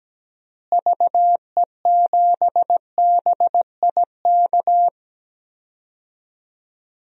Morse code, heard as a keyed tone, about 17 wpm, 700 Hz. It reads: VE7BIK
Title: DXCC-100 Call Signs - 6 Characters - Repeated 3x - 17wpm